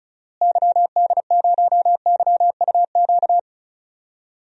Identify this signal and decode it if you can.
Morse code, heard as a keyed tone, about 36 wpm, 700 Hz. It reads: YD0YUQ
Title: DXCC-100 Call Signs - 6 Characters - Repeated 2x - 35wpm